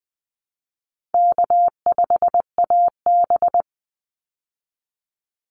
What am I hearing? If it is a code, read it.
Morse code, heard as a keyed tone, about 20 wpm, 700 Hz. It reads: K5AB
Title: Easy Call Signs Repeated 3x - 20wpm